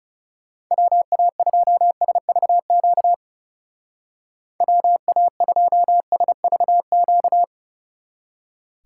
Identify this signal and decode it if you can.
Morse code, heard as a keyed tone, about 36 wpm, 700 Hz. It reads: WA2SVQ WA2SVQ
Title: Speed Racing - USA 2x3 Call Signs 30wpm